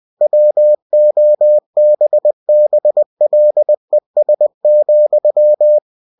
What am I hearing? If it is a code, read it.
Morse code, heard as a keyed tone, about 20 wpm, 600 Hz. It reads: WOBBLES,
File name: cw-books / winnie-the-pooh / winnie-20wpm-page-0051